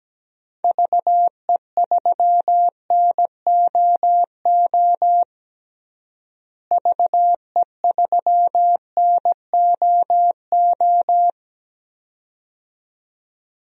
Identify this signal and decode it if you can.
Morse code, heard as a keyed tone, about 17 wpm, 700 Hz. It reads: VE3NOO VE3NOO
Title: DXCC-100 Call Signs - 6 Characters - Repeated 2x - 17wpm